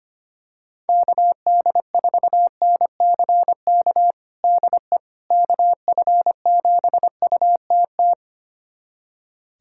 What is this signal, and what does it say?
Morse code, heard as a keyed tone, about 25 wpm, 700 Hz. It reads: KD4NCK DE KF7UTT
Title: USA Callsign DE Callsign 25wpm